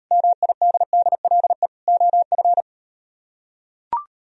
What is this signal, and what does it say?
Morse code, heard as a keyed tone, about 38 wpm, 700 Hz. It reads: MIDDLE OF